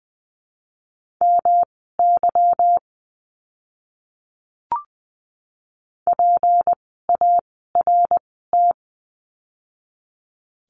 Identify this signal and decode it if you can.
Morse code, heard as a keyed tone, about 20 wpm, spaced as for 14 wpm, 700 Hz. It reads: MY PART
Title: 192 Lesson 31f - Letter v - Words 20wpm